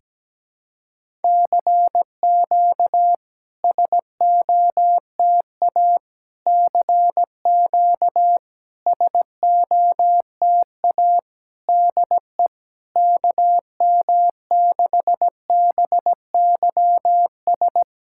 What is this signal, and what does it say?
Morse code, heard as a keyed tone, about 17 wpm, 660 Hz. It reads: CQ SOTA CQ SOTA DE KM6BYS
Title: SOTA QSO Practice 17wpm